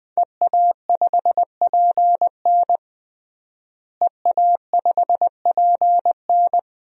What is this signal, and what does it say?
Morse code, heard as a keyed tone, about 20 wpm, 700 Hz. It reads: EA5PN EA5PN
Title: DXCC-100 Call Signs - 5 Characters - Repeated 2x - 20wpm